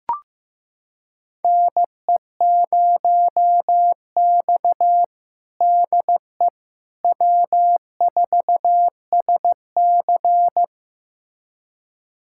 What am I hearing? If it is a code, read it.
Morse code, heard as a keyed tone, about 16 wpm, 700 Hz. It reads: NE0X DE W4SC
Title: USA Callsign DE Callsign - 4 Characters - 15wpm